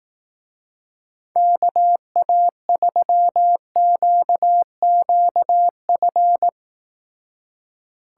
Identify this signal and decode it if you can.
Morse code, heard as a keyed tone, about 18 wpm, 700 Hz. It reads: KA3QQF
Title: Speed Racing - USA 2x3 Call Signs 15wpm